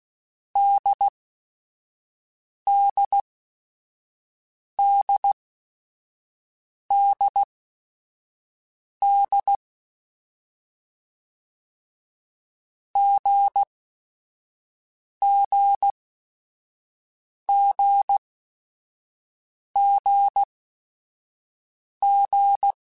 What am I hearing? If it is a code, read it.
Morse code, heard as a keyed tone, about 16 wpm, spaced as for 5 wpm, 800 Hz. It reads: DDDDD GGGGG
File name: Basic Sets / Basic Letter Groups Collective7 / AWUNDG@5